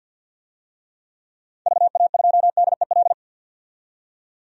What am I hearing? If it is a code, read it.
Morse code, heard as a keyed tone, about 52 wpm, 700 Hz. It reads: VA2DEL
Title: DXCC-100 Call Signs - 6 Characters - 50wpm